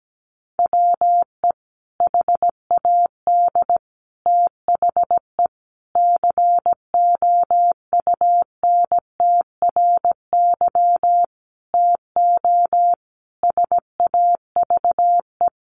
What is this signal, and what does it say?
Morse code, heard as a keyed tone, about 17 wpm, 700 Hz. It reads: WE HAD THE COUNTRY TO SAVE
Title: Sentences from Top 1100 Words 17wpm